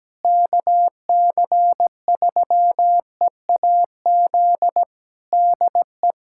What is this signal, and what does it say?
Morse code, heard as a keyed tone, about 17 wpm, 700 Hz. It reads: KC3EAZ DE
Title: Practice QSO - Exchange 4 - Ending - 17wpm